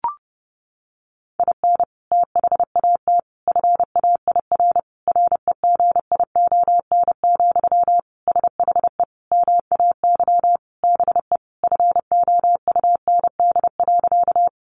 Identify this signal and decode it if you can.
Morse code, heard as a keyed tone, about 30 wpm, 700 Hz. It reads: IN THAT FAIR REGION, SHE MAY BE FOUND.
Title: Sentences from Top 1500 Words 30wpm